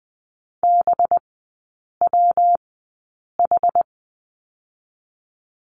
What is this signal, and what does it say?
Morse code, heard as a keyed tone, about 20 wpm, spaced as for 8 wpm, 700 Hz. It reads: BWH
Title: 184 Lesson 30e - Letter b - Three Characters 20wpm